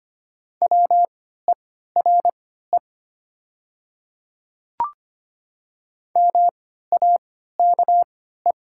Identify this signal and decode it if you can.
Morse code, heard as a keyed tone, about 25 wpm, spaced as for 14 wpm, 700 Hz. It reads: WERE MAKE
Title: 3x Character Spacing - Top 100 Words - 25wpm